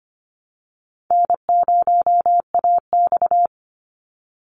Morse code, heard as a keyed tone, about 24 wpm, 700 Hz. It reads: N0AX
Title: Easy Call Signs Repeated 2x - 25wpm